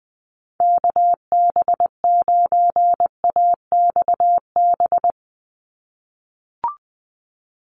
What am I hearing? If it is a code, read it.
Morse code, heard as a keyed tone, about 20 wpm, 700 Hz. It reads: KB9AXB
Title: USA Call Signs 20wpm